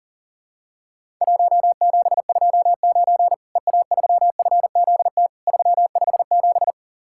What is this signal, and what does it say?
Morse code, heard as a keyed tone, about 43 wpm, 700 Hz. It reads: 1729 EA3FZT 357
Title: Worked All Europe DX Contest - QTC Exchange - 40wpm